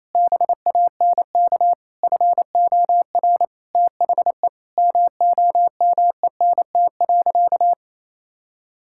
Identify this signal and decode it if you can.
Morse code, heard as a keyed tone, about 28 wpm, 700 Hz. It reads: BANK FOR THE MOMENT.
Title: Sentences from Top 1000 Words - Review Part 6 - 28wpm